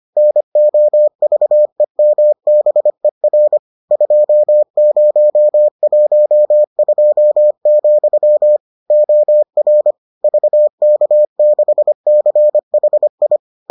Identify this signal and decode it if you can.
Morse code, heard as a keyed tone, about 25 wpm, 600 Hz. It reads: NOVEMBER 2012, OR VK6CHI